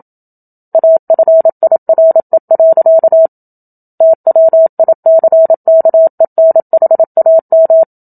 Morse code, heard as a keyed tone, about 27 wpm, 650 Hz. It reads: AFIRE. TWICKENHAM
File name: Book Two / 01-Chapter 1 - 27 WPM